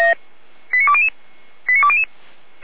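Five-tone selective call (ZVEI), heard as quick runs of five tones, tones of about 70 ms. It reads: 89200 89200
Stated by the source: ZVEI five-tone selective call received on an SDR, fire brigade alerting.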